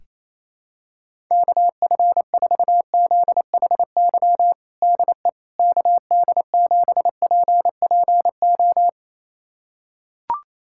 Morse code, heard as a keyed tone, about 28 wpm, 700 Hz. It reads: KF4ZHY DE KD7PPO
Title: USA Callsign DE Callsign 28wpm